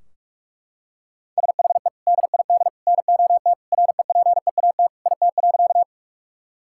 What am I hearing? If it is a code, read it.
Morse code, heard as a keyed tone, about 42 wpm, 700 Hz. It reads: SHE DID NOT REPEAT IT.
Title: Sentences from Top 1600 Words 45wpm